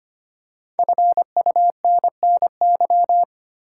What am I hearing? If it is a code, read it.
Morse code, heard as a keyed tone, about 25 wpm, 700 Hz. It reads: FUNNY
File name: 5 Letter Words - First 3 Letters Given - 25wpm